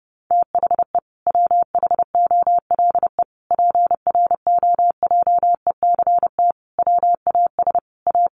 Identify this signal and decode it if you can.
Morse code, heard as a keyed tone, about 30 wpm, 700 Hz. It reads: THE WHOLE PROJECT WAS A
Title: Sentences from Top 1400 Words 30wpm